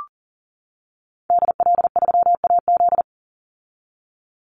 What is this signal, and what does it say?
Morse code, heard as a keyed tone, about 40 wpm, 700 Hz. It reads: DL3AZ
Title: Call Signs - 40wpm